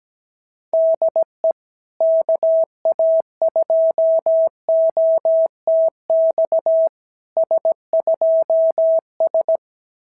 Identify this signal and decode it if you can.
Morse code, heard as a keyed tone, about 18 wpm, 650 Hz. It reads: DE KA2OTX S2S
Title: SOTA QSO Practice 17wpm